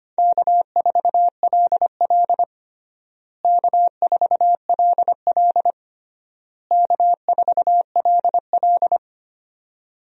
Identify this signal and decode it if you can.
Morse code, heard as a keyed tone, about 25 wpm, 700 Hz. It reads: K4LL K4LL K4LL
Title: USA Call Signs - 4 Characters - Repeated 3x - 25wpm